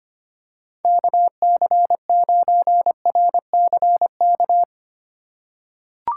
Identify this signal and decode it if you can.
Morse code, heard as a keyed tone, about 25 wpm, 700 Hz. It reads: KC9RCK
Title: USA Call Signs - 6 Characters - 3x Character Spacing - 25wpm